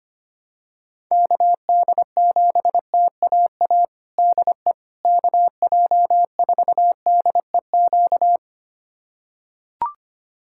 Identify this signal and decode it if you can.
Morse code, heard as a keyed tone, about 25 wpm, 700 Hz. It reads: KD7TAA DE KJ4DEQ
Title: USA Callsign DE Callsign 25wpm